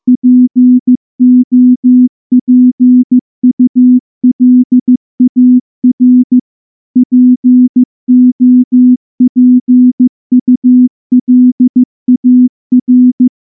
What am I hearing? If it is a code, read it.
Morse code, heard as a keyed tone, about 14 wpm, 250 Hz. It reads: POPULAR POPULAR
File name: morse-training-15WPM